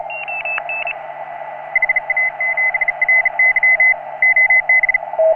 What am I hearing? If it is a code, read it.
Morse code, heard as a keyed tone, about 33 wpm, 2050 Hz, with noise. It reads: HA7RY OB
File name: CW / HA3OU RHOBCW REC